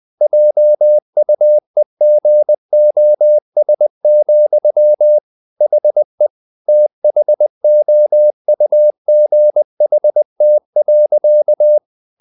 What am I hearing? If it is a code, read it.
Morse code, heard as a keyed tone, about 20 wpm, 600 Hz. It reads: JUEGOS, HE THOUGHT.
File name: the old man and the sea / the old man and the sea part 1 20wpm